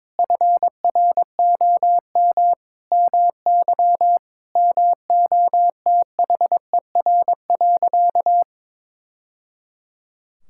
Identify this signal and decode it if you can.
Morse code, heard as a keyed tone, about 22 wpm, 700 Hz. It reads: FROM MY MOTHER.
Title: Sentences from Top 1000 Words - Review Part 6 - 22wpm